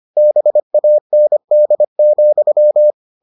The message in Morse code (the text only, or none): BAND,